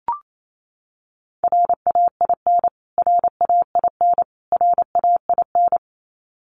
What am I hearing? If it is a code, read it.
Morse code, heard as a keyed tone, about 28 wpm, 700 Hz. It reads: RAIN RAIN RAIN
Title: Top 500 Words Repeated 28wpm